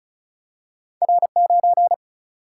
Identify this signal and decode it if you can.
Morse code, heard as a keyed tone, about 36 wpm, 700 Hz. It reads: R 9
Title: Single-Letter Single-Number 35wpm